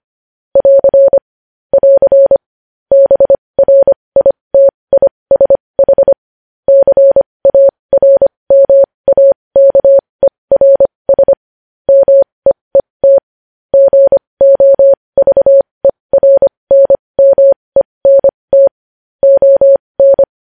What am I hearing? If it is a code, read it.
Morse code, heard as a keyed tone, about 25 wpm, spaced as for 20 wpm, 550 Hz. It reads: <AR> <AR> BRITISH CARMAKERS MEET GOVERNMENT ON
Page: news-headlines-in-morse-code-at-20-wpm